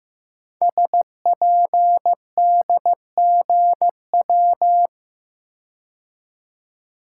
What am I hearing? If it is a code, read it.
Morse code, heard as a keyed tone, about 15 wpm, 700 Hz. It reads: SPDGW
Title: Code Groups - 5 Letters - 15wpm